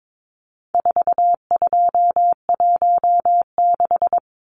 Morse code, heard as a keyed tone, about 22 wpm, 700 Hz. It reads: 4216
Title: Numbers - 4-Digit 22wpm